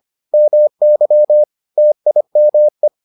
Morse code, heard as a keyed tone, about 25 wpm, 600 Hz. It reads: MY TIME